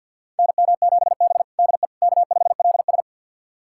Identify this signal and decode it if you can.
Morse code, heard as a keyed tone, about 51 wpm, 700 Hz. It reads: NM7B DE K5LS
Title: USA Callsign DE Callsign - 4 Characters - 50wpm